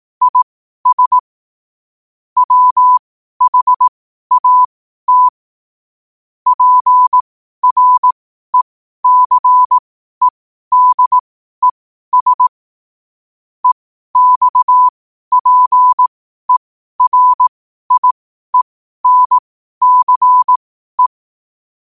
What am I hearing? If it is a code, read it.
Morse code, heard as a keyed tone, about 18 wpm, spaced as for 12 wpm, 1000 Hz. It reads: IS WHAT PRECEDES EXPERIENCE
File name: qotdfiles / 12wpm